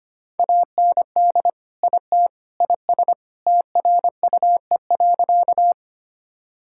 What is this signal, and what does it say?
Morse code, heard as a keyed tone, about 25 wpm, 700 Hz. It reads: AND IT IS TRUE.
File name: practice / 25wpm / Sentences with 3-Word Phrases - 25wpm